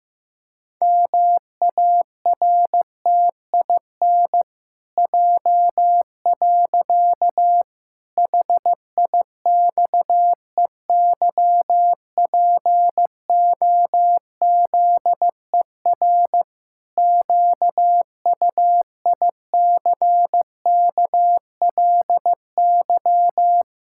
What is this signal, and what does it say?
Morse code, heard as a keyed tone, about 15 wpm, 700 Hz. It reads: MARTIN J. HIXEYPOZER QUICKLY
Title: Pangram Challenge v2 - 15wpm